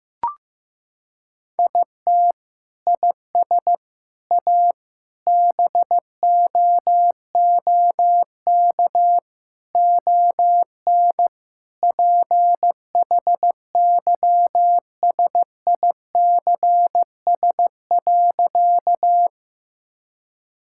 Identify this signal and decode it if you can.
Morse code, heard as a keyed tone, about 15 wpm, 700 Hz. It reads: IT IS A BOOK ON PHYSICS.
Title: Sentences from Top 1000 Words - Review Part 5 - 15wpm